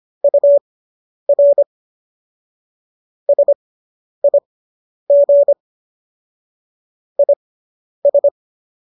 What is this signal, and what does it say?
Morse code, heard as a keyed tone, about 25 wpm, spaced as for 10 wpm, 570 Hz. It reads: UR SIG IS